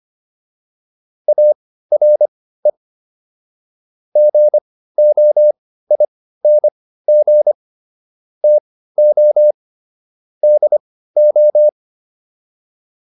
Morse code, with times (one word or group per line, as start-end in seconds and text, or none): 1.3-2.7 ARE
4.2-7.5 GOING
8.4-9.5 TO
10.4-11.7 DO